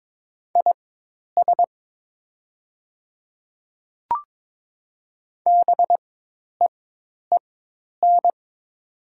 Morse code, heard as a keyed tone, about 22 wpm, spaced as for 10 wpm, 700 Hz. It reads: IS BEEN
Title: 4x Character Spacing - Top 200 Words - 22wpm